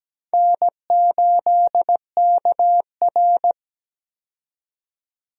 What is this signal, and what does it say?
Morse code, heard as a keyed tone, about 17 wpm, 700 Hz. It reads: N8KR
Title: USA Call Signs - 4 Characters - 17wpm